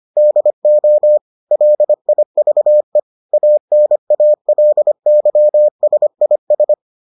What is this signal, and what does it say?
Morse code, heard as a keyed tone, about 25 wpm, 600 Hz. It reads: DO LIVE ANALYSIS